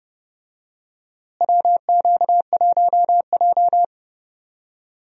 Morse code, heard as a keyed tone, about 30 wpm, 700 Hz. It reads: WQ1J
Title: USA Call Signs - 4 Characters - Repeated 3x - 30wpm